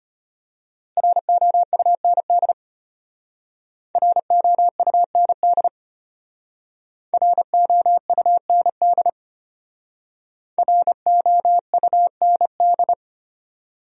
Morse code, heard as a keyed tone, about 32 wpm, 700 Hz. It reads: ROUND ROUND ROUND ROUND